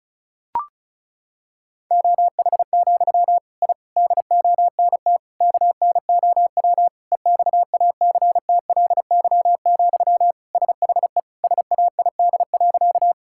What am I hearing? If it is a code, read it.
Morse code, heard as a keyed tone, about 36 wpm, 700 Hz. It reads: OH, I DONT KNOW EXACTLY, SHE SAID.
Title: Sentences from Top 1000 Words - Review Part 6 - 35wpm